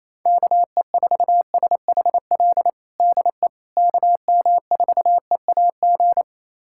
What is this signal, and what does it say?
Morse code, heard as a keyed tone, about 28 wpm, 700 Hz. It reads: KE4SHL DE KM4EAG
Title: USA Callsign DE Callsign - 6 Characters - 28wpm